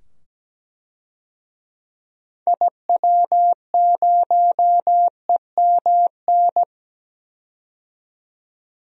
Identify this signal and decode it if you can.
Morse code, heard as a keyed tone, about 17 wpm, 700 Hz. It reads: IW0EMN
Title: DXCC-100 Call Signs - 6 Characters - 17wpm